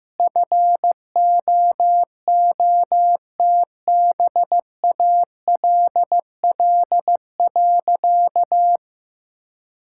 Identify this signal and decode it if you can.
Morse code, heard as a keyed tone, about 14 wpm, 700 Hz. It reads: FOOTBALL.
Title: Sentences from Top 1600 Words 15wpm